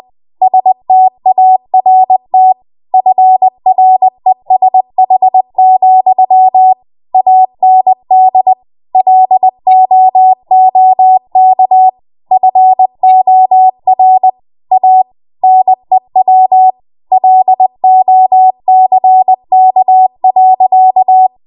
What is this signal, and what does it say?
Morse code, heard as a keyed tone, about 20 wpm, 750 Hz. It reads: START FRESH, AND LOOK FOR A NEW LOCK.
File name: Morse / 20WPM